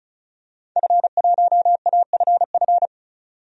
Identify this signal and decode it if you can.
Morse code, heard as a keyed tone, about 34 wpm, 700 Hz. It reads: F1AFF
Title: DXCC-100 Call Signs - 5 Characters - 35wpm